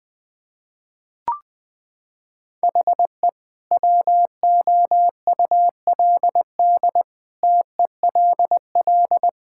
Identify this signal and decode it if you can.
Morse code, heard as a keyed tone, about 20 wpm, 700 Hz. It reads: HE WOULD TELL
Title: Sentences from Top 1600 Words 20wpm